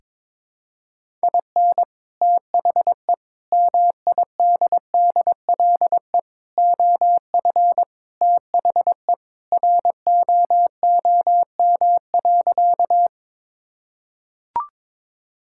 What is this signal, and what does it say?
Morse code, heard as a keyed tone, about 21 wpm, 700 Hz. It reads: IN THE MIDDLE OF THE ROOM.
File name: practice / 22wpm / Sentences with 2-Word Phrases - 22wpm